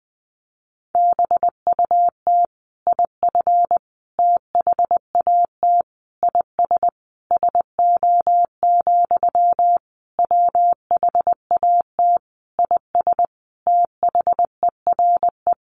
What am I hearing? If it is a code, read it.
Morse code, heard as a keyed tone, about 20 wpm, 700 Hz. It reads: BUT IF THAT IS SO, WHAT IS THERE
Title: Sentences from Top 200 Words 20wpm